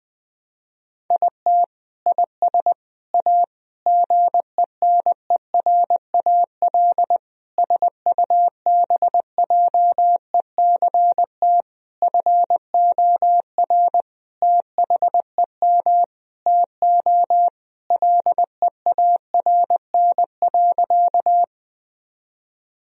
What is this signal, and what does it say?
Morse code, heard as a keyed tone, about 21 wpm, 700 Hz. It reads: IT IS A GENERAL SUBJECT FOR THEM TO LEARN.
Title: Sentences from Top 1000 Words - Review Part 3 - 20wpm